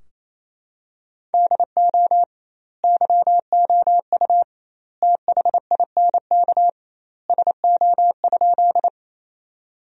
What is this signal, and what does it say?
Morse code, heard as a keyed tone, about 28 wpm, 700 Hz. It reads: DO YOU THINK SO?
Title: 2x Word-Spacing - Sentences from Top 200 Words - 28wpm